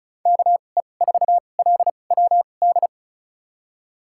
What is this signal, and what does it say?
Morse code, heard as a keyed tone, about 36 wpm, spaced as for 25 wpm, 700 Hz. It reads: KE4LWD
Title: USA Call Signs - 6 Characters - 2x Character Spacing - 35wpm